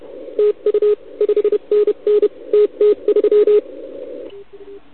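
Morse code, heard as a keyed tone, about 32 wpm, 420 Hz. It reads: TU 5NN TT3
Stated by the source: CW from the OKOM DX contest on 80 m, received with a TS480.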